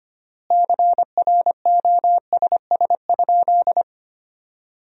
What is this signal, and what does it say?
Morse code, heard as a keyed tone, about 25 wpm, 700 Hz. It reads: CROSS?
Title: 3x Word-Spacing - Sentences from Top 300 Words - 25wpm